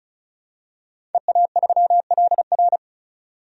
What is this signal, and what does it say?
Morse code, heard as a keyed tone, about 34 wpm, 700 Hz. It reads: EA3LR
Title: DXCC-100 Call Signs - 5 Characters - 35wpm